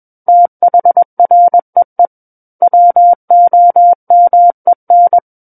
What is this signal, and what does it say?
Morse code, heard as a keyed tone, about 21 wpm, 700 Hz. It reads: THREE WOMEN
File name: Book One / 12-Chapter 12 - 21 WPM